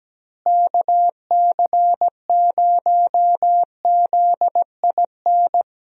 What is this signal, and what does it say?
Morse code, heard as a keyed tone, about 17 wpm, 700 Hz. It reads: KC0ZIN
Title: USA Call Signs - 6 Characters - Repeated 3x - 17wpm